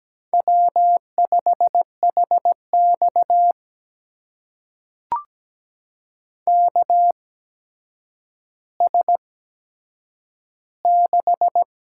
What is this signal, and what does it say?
Morse code, heard as a keyed tone, about 17 wpm, spaced as for 8 wpm, 700 Hz. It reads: W5HX KS6
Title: USA Call Signs - 8x Character Spacing - 17wpm